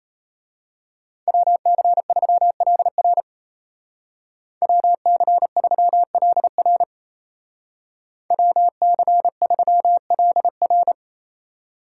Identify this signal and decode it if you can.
Morse code, heard as a keyed tone, about 34 wpm, 700 Hz. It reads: WC3LR WC3LR WC3LR
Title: Speed Racing - USA 2x2 Call Signs 28wpm